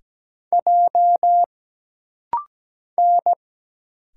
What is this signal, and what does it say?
Morse code, heard as a keyed tone, about 17 wpm, 700 Hz. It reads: J N
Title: Single Letter-Number - Rapid-Fire 17wpm